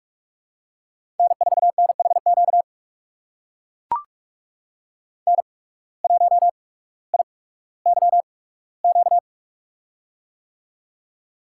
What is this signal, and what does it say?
Morse code, heard as a keyed tone, about 44 wpm, spaced as for 19 wpm, 700 Hz. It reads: N4NHQ N1IYQ
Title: USA Call Signs - 5 Characters - 8x Character Spacing - 45wpm